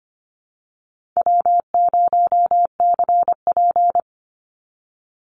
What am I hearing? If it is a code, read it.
Morse code, heard as a keyed tone, about 25 wpm, 700 Hz. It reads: W0CP
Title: NA SOTA Chaser Callsigns 25wpm